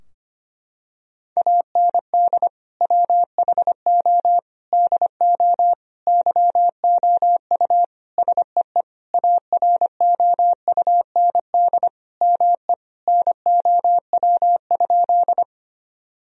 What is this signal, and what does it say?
Morse code, heard as a keyed tone, about 25 wpm, 700 Hz. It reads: AND WHO DO YOU SEE AROUND ME NOW?
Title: Sentences from Top 1000 Words - Review Part 3 - 25wpm